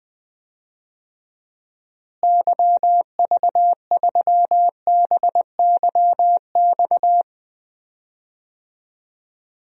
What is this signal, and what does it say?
Morse code, heard as a keyed tone, about 20 wpm, 700 Hz. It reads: YV3BYX
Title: DXCC-100 Call Signs - 6 Characters - Repeated 2x - 20wpm